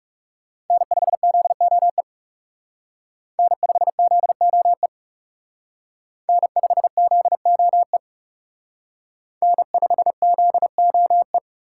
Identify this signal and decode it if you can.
Morse code, heard as a keyed tone, about 37 wpm, 700 Hz. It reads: N5ZOE N5ZOE N5ZOE N5ZOE
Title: Speed Racing - USA Call Signs 30wpm